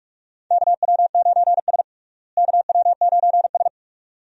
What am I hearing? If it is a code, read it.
Morse code, heard as a keyed tone, about 45 wpm, 700 Hz. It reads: KW9S KW9S
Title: USA Call Signs - 4 Characters - Repeated 2x - 45wpm